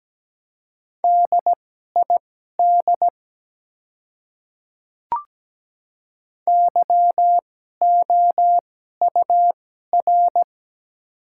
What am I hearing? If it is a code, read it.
Morse code, heard as a keyed tone, about 17 wpm, spaced as for 12 wpm, 700 Hz. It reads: DID YOUR
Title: 2x Character Spacing - Top 100 Words - 17wpm